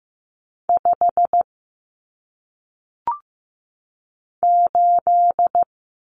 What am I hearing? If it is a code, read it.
Morse code, heard as a keyed tone, about 15 wpm, 700 Hz. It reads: 5 8
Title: Numbers - 1-Digit 15wpm